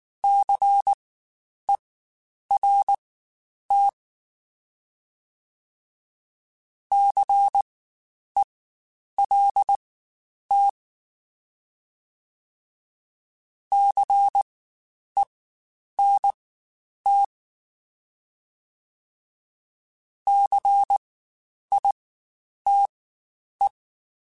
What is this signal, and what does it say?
Morse code, heard as a keyed tone, about 19 wpm, spaced as for 9 wpm, 800 Hz. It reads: CERT CELT CENT CITE
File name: Spaced Alphas / Spaced Alphas mp3s / Cs sp @18